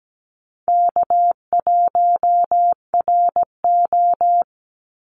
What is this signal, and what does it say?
Morse code, heard as a keyed tone, about 17 wpm, 700 Hz. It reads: K1RO
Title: Easy Call Signs Repeated 3x - 17wpm